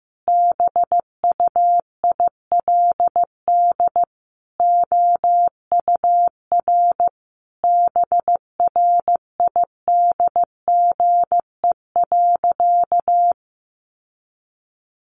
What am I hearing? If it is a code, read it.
Morse code, heard as a keyed tone, about 14 wpm, 700 Hz. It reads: BUILD OUR BRIDGE.
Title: Sentences from Top 1200 Words 15wpm